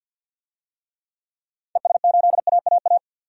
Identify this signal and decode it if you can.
Morse code, heard as a keyed tone, about 52 wpm, 700 Hz. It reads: ES8AAA
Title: DXCC-100 Call Signs - 6 Characters - 50wpm